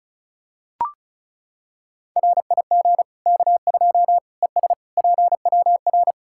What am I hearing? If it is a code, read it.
Morse code, heard as a keyed tone, about 34 wpm, 700 Hz. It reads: RIG K2 ES PWR
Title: Practice QSO - Exchange 2 - Info - 35wpm